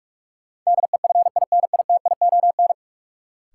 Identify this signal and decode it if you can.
Morse code, heard as a keyed tone, about 44 wpm, 700 Hz. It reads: DEFINITION
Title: Sentences from Top 1000 Words - Review Part 4 - New Word Drill - 45wpm